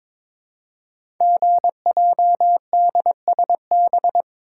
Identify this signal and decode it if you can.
Morse code, heard as a keyed tone, about 22 wpm, 700 Hz. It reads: GJDSB
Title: Code Groups - 5 Letters - 22wpm